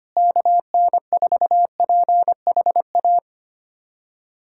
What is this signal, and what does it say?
Morse code, heard as a keyed tone, about 25 wpm, 700 Hz. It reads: KN4PHA
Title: USA Callsign DE Callsign 25wpm